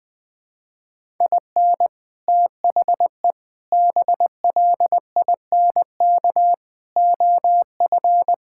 Morse code, heard as a keyed tone, about 21 wpm, 700 Hz. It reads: IN THE BLINK OF
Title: Sets of 6 Words - 20wpm